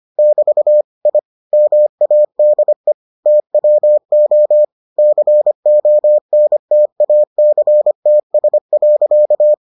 Morse code, heard as a keyed tone, about 25 wpm, 600 Hz. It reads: <BT> I MADE TWO CONTACTS.